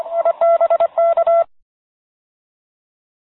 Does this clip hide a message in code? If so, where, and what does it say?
Yes: Morse code from 0.2 to 1.4 s, heard as a keyed tone, about 25 wpm, 670 Hz. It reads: IBK